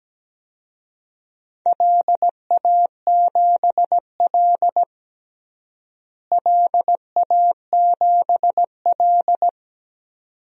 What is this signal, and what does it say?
Morse code, heard as a keyed tone, about 17 wpm, 700 Hz. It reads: LA7L LA7L
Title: DXCC-100 Call Signs - 4 Characters - Repeated 2x - 17wpm